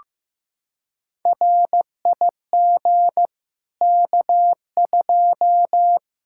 Morse code, heard as a keyed tone, about 15 wpm, 700 Hz. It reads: RIG K2
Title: Practice QSO - Exchange 2 - Info - 15wpm